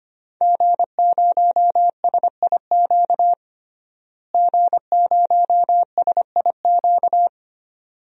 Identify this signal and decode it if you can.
Morse code, heard as a keyed tone, about 25 wpm, 700 Hz. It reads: G0SIQ G0SIQ
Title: DXCC-100 Call Signs - 5 Characters - Repeated 2x - 25wpm